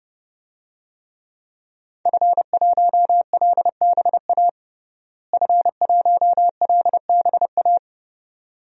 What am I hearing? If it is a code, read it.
Morse code, heard as a keyed tone, about 30 wpm, 700 Hz. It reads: F1LBA F1LBA
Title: DXCC-100 Call Signs - 5 Characters - Repeated 2x - 30wpm